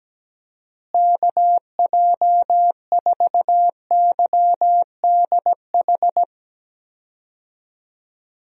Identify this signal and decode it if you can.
Morse code, heard as a keyed tone, about 17 wpm, 700 Hz. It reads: KJ4YDH
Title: USA Call Signs - 6 Characters - Repeated 2x - 17wpm